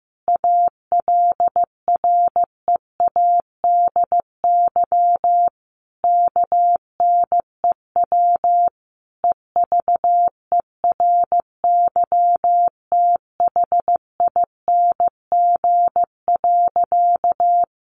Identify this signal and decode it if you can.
Morse code, heard as a keyed tone, about 15 wpm, 700 Hz. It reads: ALREADY KNEW EVERYTHING.
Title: Sentences from Top 700 Words 15wpm